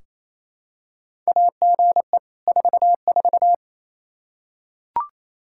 Morse code, heard as a keyed tone, about 27 wpm, 700 Hz. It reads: AGE 44
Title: Practice QSO - Exchange 3 - More Info - 28wpm